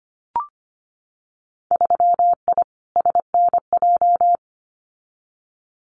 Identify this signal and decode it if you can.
Morse code, heard as a keyed tone, about 25 wpm, 700 Hz. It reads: 3I SNJ
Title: Winter Field Day Exchanges Encore 25wpm